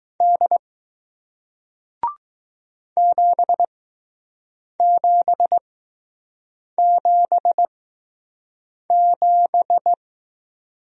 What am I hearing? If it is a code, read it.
Morse code, heard as a keyed tone, about 20 wpm, spaced as for 6 wpm, 700 Hz. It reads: D 7777